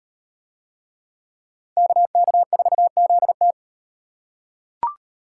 Morse code, heard as a keyed tone, about 37 wpm, 700 Hz. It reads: KK4ZT